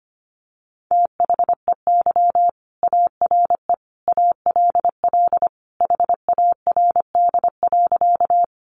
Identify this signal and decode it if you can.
Morse code, heard as a keyed tone, about 25 wpm, 700 Hz. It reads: THEY ARE ALL HARD.
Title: Additional Easy Phrases 25wpm